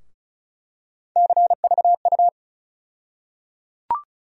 Morse code, heard as a keyed tone, about 36 wpm, 700 Hz. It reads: CVU
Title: USA Call Sign Postfixes - 3 Letters 35wpm